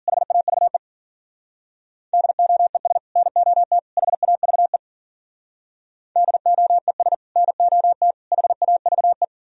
Morse code, heard as a keyed tone, about 52 wpm, 700 Hz. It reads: HAVE DOES NOT HAVE DOES NOT HAVE